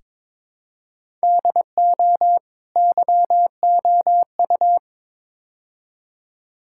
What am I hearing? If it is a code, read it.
Morse code, heard as a keyed tone, about 22 wpm, 700 Hz. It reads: DO YOU
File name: practice / 22wpm / Build A Phrase - 5-Word Phrases - 22wpm